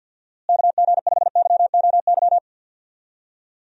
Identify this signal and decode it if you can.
Morse code, heard as a keyed tone, about 51 wpm, 700 Hz. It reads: KG5YOY
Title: USA Call Signs - 6 Characters - Repeated 3x - 50wpm